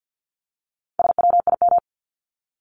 Morse code, heard as a keyed tone, about 50 wpm, 700 Hz. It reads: SWIM
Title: Top 1751-2000 Words 50wpm